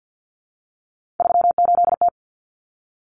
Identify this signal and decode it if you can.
Morse code, heard as a keyed tone, about 50 wpm, 700 Hz. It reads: 38T
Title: Common Cut Numbers - 3-Digit 50wpm